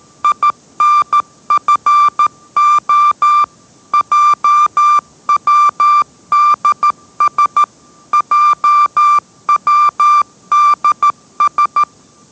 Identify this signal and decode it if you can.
Morse code, heard as a keyed tone, about 13 wpm, 1180 Hz. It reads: INFO JWDS JWDS